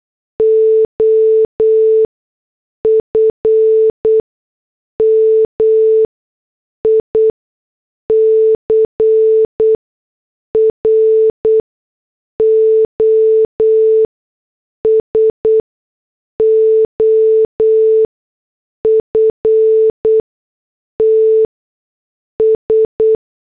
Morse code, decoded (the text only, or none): OF MICROSOFTS